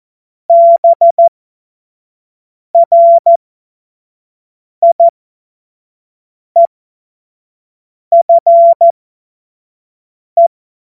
Morse code, heard as a keyed tone, about 14 wpm, spaced as for 5 wpm, 680 Hz. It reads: BRIEFE
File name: Morsekurs / Witze / Witz 20. Juli 2021, 25 F70